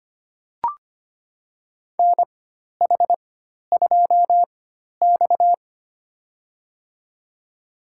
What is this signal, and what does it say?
Morse code, heard as a keyed tone, about 25 wpm, spaced as for 12 wpm, 700 Hz. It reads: NH2X
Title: USA Call Signs - 4 Characters - 4x Character Spacing - 25wpm